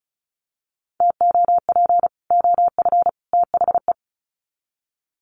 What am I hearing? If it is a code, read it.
Morse code, heard as a keyed tone, about 34 wpm, 700 Hz. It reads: TOP OF THE
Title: Sets of 3 Words 35wpm